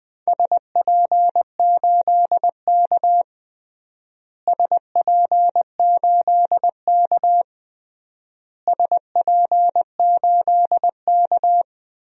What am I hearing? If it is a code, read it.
Morse code, heard as a keyed tone, about 20 wpm, 700 Hz. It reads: SP8K SP8K SP8K
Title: DXCC-100 Call Signs - 4 Characters - Repeated 3x - 20wpm